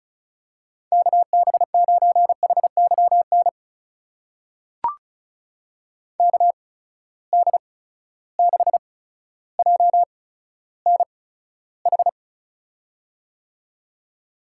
Morse code, heard as a keyed tone, about 36 wpm, spaced as for 15 wpm, 700 Hz. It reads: KB9HYN KD6JNH
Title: USA Call Signs - 6 Characters - 8x Character Spacing - 35wpm